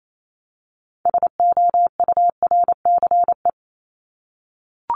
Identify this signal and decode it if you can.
Morse code, heard as a keyed tone, about 28 wpm, 700 Hz. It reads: SOURCE
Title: ES:S Suffix - Top 100 Words with Base Words 28wpm